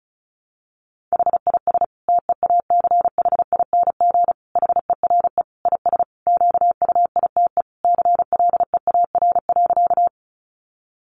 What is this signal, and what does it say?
Morse code, heard as a keyed tone, about 34 wpm, 700 Hz. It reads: HIS TEACHING HERE IS QUITE CLEAR.
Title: Sentences from Top 1100 Words 35wpm